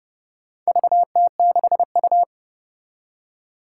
Morse code, heard as a keyed tone, about 30 wpm, 690 Hz. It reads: VT6U
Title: Four Challenging Letter-Number Combinations 30wpm